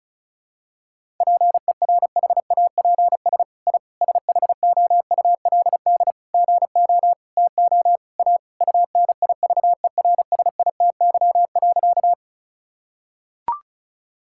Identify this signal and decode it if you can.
Morse code, heard as a keyed tone, about 36 wpm, 700 Hz. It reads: PERHAPS I SHOULD GO TO A UNIVERSITY.
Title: Sentences from Top 1000 Words - Review Part 5 - 35wpm